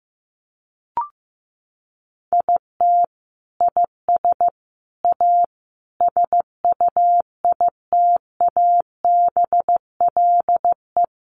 Morse code, heard as a keyed tone, about 15 wpm, 700 Hz. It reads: IT IS A SUITABLE
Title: Sentences from Top 1400 Words 15wpm